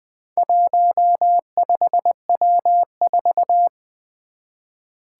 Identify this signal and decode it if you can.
Morse code, heard as a keyed tone, about 20 wpm, 700 Hz. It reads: 15W4
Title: Four Challenging Letter-Number Combinations 20wpm